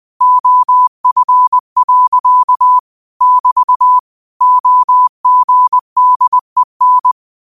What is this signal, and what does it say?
Morse code, heard as a keyed tone, about 20 wpm, 1000 Hz. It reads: OF. <BT> OGDEN
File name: qotdfiles / 20wpm